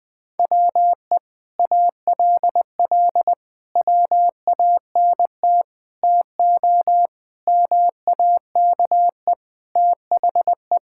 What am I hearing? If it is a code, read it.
Morse code, heard as a keyed tone, about 21 wpm, 700 Hz. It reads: WE ALL WANT TO MAKE THE
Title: Sentences with 7-Word Phrases - 20wpm